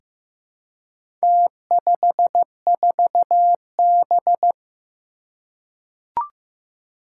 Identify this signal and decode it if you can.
Morse code, heard as a keyed tone, about 15 wpm, 700 Hz. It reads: T54B
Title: Four Challenging Letter-Number Combinations 15wpm